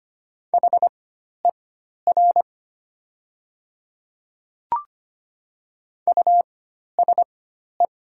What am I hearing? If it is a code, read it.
Morse code, heard as a keyed tone, about 25 wpm, spaced as for 12 wpm, 690 Hz. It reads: HER USE
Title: 4x Character Spacing - Top 100 Words - 25wpm